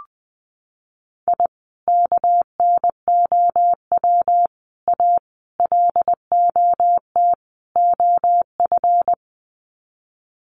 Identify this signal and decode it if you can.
Morse code, heard as a keyed tone, about 20 wpm, 700 Hz. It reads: I KNOW A LOT OF
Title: Sets of 5 Words Encore 20wpm